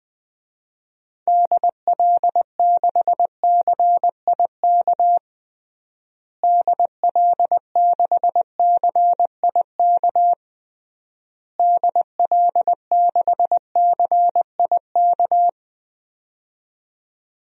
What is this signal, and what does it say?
Morse code, heard as a keyed tone, about 20 wpm, 700 Hz. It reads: DL6CIK DL6CIK DL6CIK
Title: DXCC-100 Call Signs - 6 Characters - Repeated 3x - 20wpm